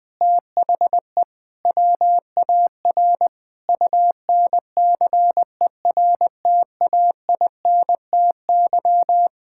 Morse code, heard as a keyed tone, about 20 wpm, 700 Hz. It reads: THE WAR UNCERTAINTY
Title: Sentences with 6-Word Phrases - 20wpm